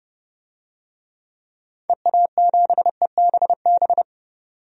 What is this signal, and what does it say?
Morse code, heard as a keyed tone, about 32 wpm, 700 Hz. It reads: EA7EBB
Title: DXCC-100 Call Signs - 6 Characters - 30wpm